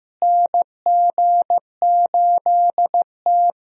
Morse code, heard as a keyed tone, about 15 wpm, 700 Hz. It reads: NG8T
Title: USA Call Signs - 4 Characters - Repeated 2x - 15wpm